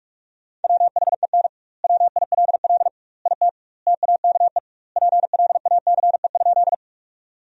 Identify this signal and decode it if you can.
Morse code, heard as a keyed tone, about 42 wpm, 700 Hz. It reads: WHEN WILL IT TAKE PLACE?
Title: 2x Word-Spacing - Sentences from Top 200 Words - 45wpm